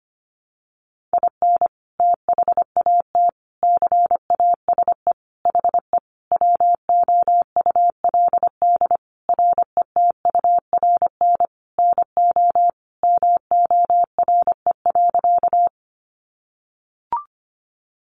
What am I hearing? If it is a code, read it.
Morse code, heard as a keyed tone, about 25 wpm, 700 Hz. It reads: IN THAT CASE HE WOULD RETURN NO MORE.
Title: Sentences from Top 800 Words 25wpm